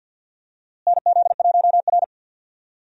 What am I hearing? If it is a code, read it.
Morse code, heard as a keyed tone, about 46 wpm, 700 Hz. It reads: NZ1R
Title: USA Call Signs - 4 Characters - Repeated 3x - 50wpm